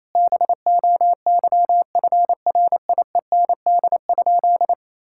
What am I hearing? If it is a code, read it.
Morse code, heard as a keyed tone, about 27 wpm, 700 Hz. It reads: BOYFRIEND?
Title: Sentences from Top 1000 Words - Review Part 5 - 28wpm